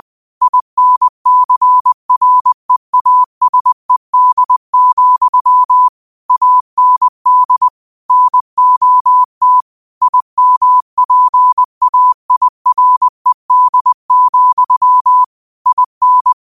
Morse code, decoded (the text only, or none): INCREASED, AND NOT IMPAIRED, IN